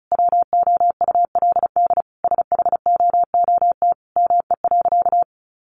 Morse code, heard as a keyed tone, about 34 wpm, 700 Hz. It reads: WOULD SHOOT ME.
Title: Sentences from Top 1500 Words 35wpm